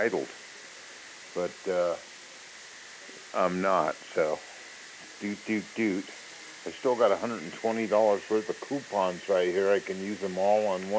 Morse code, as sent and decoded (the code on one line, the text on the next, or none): none